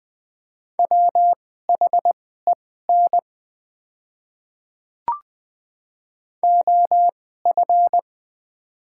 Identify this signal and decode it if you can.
Morse code, heard as a keyed tone, about 21 wpm, spaced as for 15 wpm, 700 Hz. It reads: WHEN OF
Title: 2x Character Spacing - Top 100 Words - 20wpm